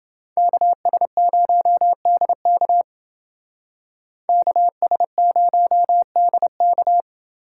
Morse code, heard as a keyed tone, about 29 wpm, 700 Hz. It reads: KS0DK KS0DK